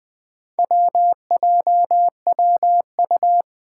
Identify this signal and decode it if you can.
Morse code, heard as a keyed tone, about 20 wpm, 700 Hz. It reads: WJWU
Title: Four Challenging Letter-Number Combinations 20wpm